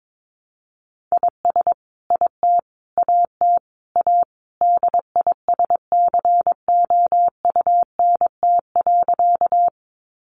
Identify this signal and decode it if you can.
Morse code, heard as a keyed tone, about 22 wpm, 700 Hz. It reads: IS IT AT A DISCOUNT.
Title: Sentences from Top 1200 Words 22wpm